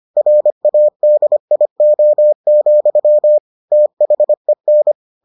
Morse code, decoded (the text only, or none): RADIO, THEN